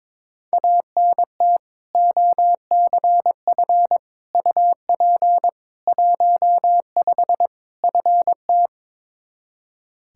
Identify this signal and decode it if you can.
Morse code, heard as a keyed tone, about 22 wpm, 700 Hz. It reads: ANT OCF UP 15 FT
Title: Practice QSO - Exchange 2 - Info - 22wpm